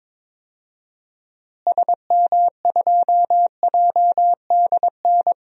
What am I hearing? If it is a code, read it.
Morse code, heard as a keyed tone, about 21 wpm, 700 Hz. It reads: SM2JDN
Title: DXCC-100 Call Signs - 6 Characters - Repeated 2x - 22wpm